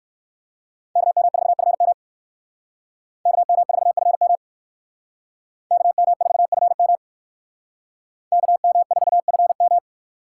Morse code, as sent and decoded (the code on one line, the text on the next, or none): -.- -- ....- ..-. -- / -.- -- ....- ..-. -- / -.- -- ....- ..-. -- / -.- -- ....- ..-. --
KM4FM KM4FM KM4FM KM4FM